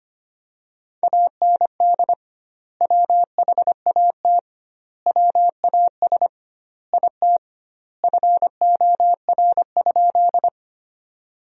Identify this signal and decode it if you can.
Morse code, heard as a keyed tone, about 25 wpm, 700 Hz. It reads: AND WHAT WAS IT FOR?
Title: 2x Word-Spacing - Sentences from Top 100 Words - 25wpm